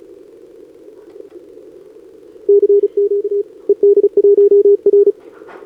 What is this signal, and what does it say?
Morse code, heard as a keyed tone, about 32 wpm, 410 Hz. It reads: CQ ED1R
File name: CQWWCW2025 / NedeleOdpol 20m Scan